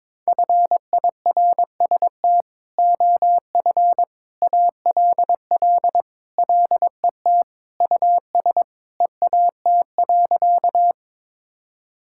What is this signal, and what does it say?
Morse code, heard as a keyed tone, about 22 wpm, 700 Hz. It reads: FIRST OF ALL LET US EAT.
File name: practice / 22wpm / Sentences with 3-Word Phrases - 22wpm